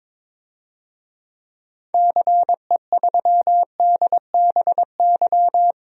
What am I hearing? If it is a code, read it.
Morse code, heard as a keyed tone, about 23 wpm, 700 Hz. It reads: CE3DBY
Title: DXCC-100 Call Signs - 6 Characters - 22wpm